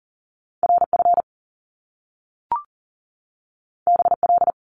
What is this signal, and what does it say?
Morse code, heard as a keyed tone, about 38 wpm, 700 Hz. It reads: RF BL
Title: Top Two Letter Combinations in Words 40wpm